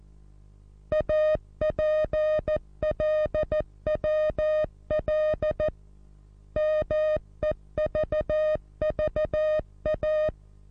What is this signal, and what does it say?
Morse code, heard as a keyed tone, about 14 wpm, 620 Hz. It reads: APLWL MEVVA
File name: repos / Morsekurs / DJ4UF Morsen27